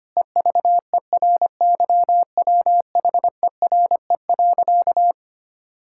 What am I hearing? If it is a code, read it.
Morse code, heard as a keyed tone, about 25 wpm, 700 Hz. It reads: EVERYWHERE.
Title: Sentences from Top 1600 Words 25wpm